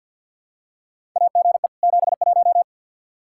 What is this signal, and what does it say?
Morse code, heard as a keyed tone, about 46 wpm, 700 Hz. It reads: AGE 71
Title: Practice QSO - Exchange 3 - More Info - 50wpm